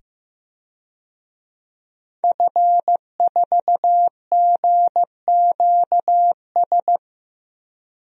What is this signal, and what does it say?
Morse code, heard as a keyed tone, about 16 wpm, 700 Hz. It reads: F4GQS
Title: DXCC-100 Call Signs - 5 Characters - 15wpm